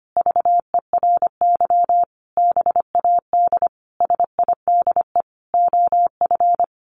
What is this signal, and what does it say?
Morse code, heard as a keyed tone, about 25 wpm, 700 Hz. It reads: VERY BAD SIDE OF
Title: Sentences from Top 700 Words 25wpm